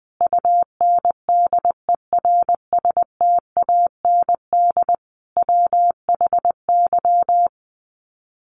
Morse code, heard as a keyed tone, about 20 wpm, 700 Hz. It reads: UNDERSTAND WHY
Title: Sets of 5 Words Encore 20wpm